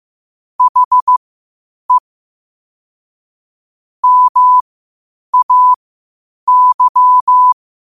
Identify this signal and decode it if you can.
Morse code, heard as a keyed tone, about 15 wpm, spaced as for 8 wpm, 1000 Hz. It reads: HE MAY
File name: qotdfiles / 8wpm